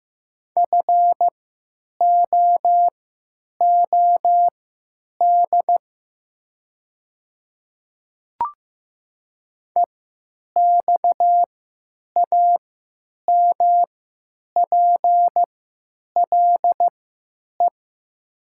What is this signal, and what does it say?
Morse code, heard as a keyed tone, about 15 wpm, spaced as for 9 wpm, 700 Hz. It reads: FOOD EXAMPLE
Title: 3x Character Spacing - Top 300 Words - 15wpm